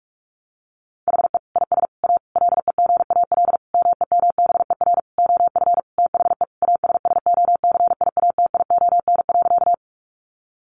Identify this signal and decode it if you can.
Morse code, heard as a keyed tone, about 44 wpm, 700 Hz. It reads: HE IS A LEGAL MEMBER OF THE ASSOCIATION.
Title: Sentences from Top 1200 Words 45wpm